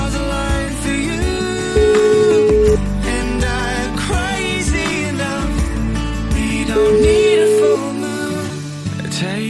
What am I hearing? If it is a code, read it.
Ringback tone (Europe, 425 Hz), with noise.